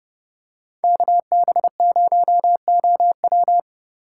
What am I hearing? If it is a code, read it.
Morse code, heard as a keyed tone, about 29 wpm, 700 Hz. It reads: KB0OW